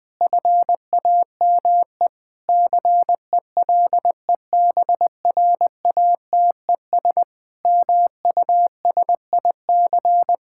Morse code, heard as a keyed tone, about 19 wpm, 700 Hz. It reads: FAME CELEBRATES MUSIC
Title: Sentences with 7-Word Phrases - 20wpm